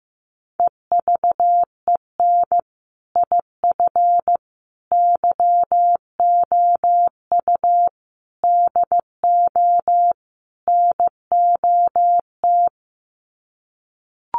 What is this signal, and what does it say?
Morse code, heard as a keyed tone, about 15 wpm, 700 Hz. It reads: EVEN IF YOU DO NOT
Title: Sets of 5 Words Encore 15wpm